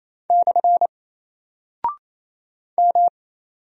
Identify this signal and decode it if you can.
Morse code, heard as a keyed tone, about 27 wpm, 700 Hz. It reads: / M
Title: Single Character (Letter:Number:Punctuation:Prosign) 28wpm